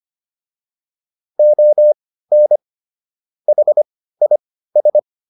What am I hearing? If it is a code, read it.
Morse code, heard as a keyed tone, about 25 wpm, spaced as for 15 wpm, 600 Hz. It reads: ON HIS